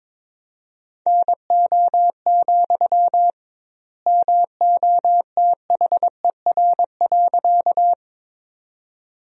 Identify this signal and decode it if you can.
Morse code, heard as a keyed tone, about 22 wpm, 700 Hz. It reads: NO, MOTHER.
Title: 2x Word-Spacing - Sentences from Top 200 Words - 22wpm